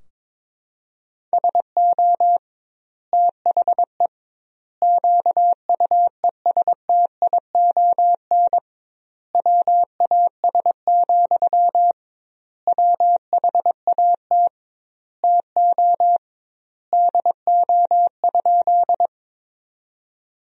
Morse code, heard as a keyed tone, about 22 wpm, 700 Hz. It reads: SO THE QUESTION WAS, WHAT TO DO?
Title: 2x Word-Spacing - Sentences from Top 400 Words - 22wpm